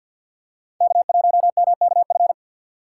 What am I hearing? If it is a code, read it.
Morse code, heard as a keyed tone, about 52 wpm, 700 Hz. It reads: K1MKF
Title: USA Call Signs - 5 Characters - 3x Character Spacing - 50wpm